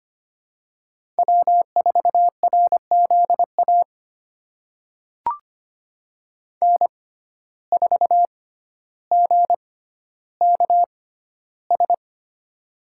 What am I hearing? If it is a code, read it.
Morse code, heard as a keyed tone, about 25 wpm, spaced as for 13 wpm, 700 Hz. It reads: W4RZA N4GKS
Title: USA Call Signs - 5 Characters - 6x Character Spacing - 25wpm